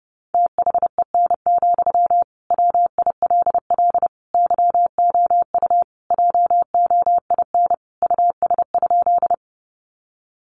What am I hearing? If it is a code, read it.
Morse code, heard as a keyed tone, about 30 wpm, 700 Hz. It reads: THEN, WILL YOU JOIN US?
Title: Sentences from Top 1400 Words 30wpm